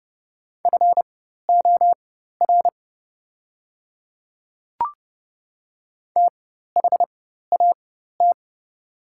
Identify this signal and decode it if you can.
Morse code, heard as a keyed tone, about 30 wpm, spaced as for 14 wpm, 700 Hz. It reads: FOR THAT
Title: 4x Character Spacing - Top 100 Words - 30wpm